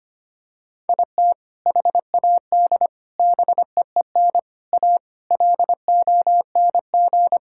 Morse code, heard as a keyed tone, about 25 wpm, 700 Hz. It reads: IT HAD BEEN A LONG
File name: practice / 25wpm / Sentences with 3-Word Phrases - 25wpm